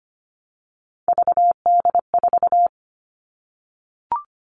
Morse code, heard as a keyed tone, about 25 wpm, 700 Hz. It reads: VD4
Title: All Cut Numbers - 3-Digit 25wpm